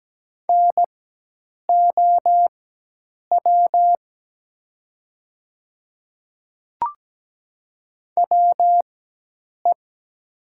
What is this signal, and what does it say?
Morse code, heard as a keyed tone, about 17 wpm, spaced as for 8 wpm, 700 Hz. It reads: NOW WE
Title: 4x Character Spacing - Top 100 Words - 17wpm